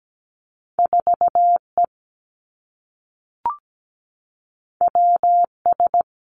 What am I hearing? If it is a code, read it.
Morse code, heard as a keyed tone, about 17 wpm, 700 Hz. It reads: 4 E W S
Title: Two Challenging Letter-Number Combinations 17wpm